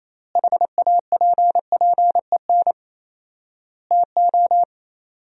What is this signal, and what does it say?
Morse code, heard as a keyed tone, about 28 wpm, 700 Hz. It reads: HAPPEN TO
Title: 4x Word-Spacing - Sentences from Top 400 Words - 28wpm